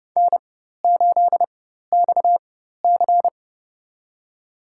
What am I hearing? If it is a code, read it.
Morse code, heard as a keyed tone, about 30 wpm, spaced as for 14 wpm, 700 Hz. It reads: N8XC
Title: USA Call Signs - 4x Character Spacing - 30wpm